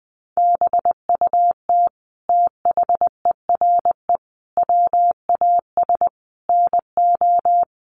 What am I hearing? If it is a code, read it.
Morse code, heard as a keyed tone, about 20 wpm, 700 Hz. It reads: BUT THERE WAS NO
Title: Sentences from Top 1500 Words 20wpm